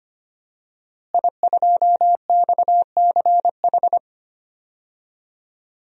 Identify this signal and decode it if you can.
Morse code, heard as a keyed tone, about 25 wpm, 700 Hz. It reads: I2XCH
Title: DXCC-100 Call Signs - 5 Characters - Repeated 3x - 25wpm